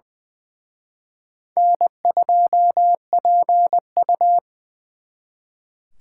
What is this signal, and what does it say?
Morse code, heard as a keyed tone, about 23 wpm, 700 Hz. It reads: N2PU
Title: Speed Racing - USA Call Signs 20wpm